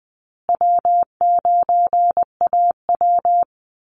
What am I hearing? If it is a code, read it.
Morse code, heard as a keyed tone, about 20 wpm, 700 Hz. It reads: W9AW
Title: Easy Call Signs Repeated 3x - 20wpm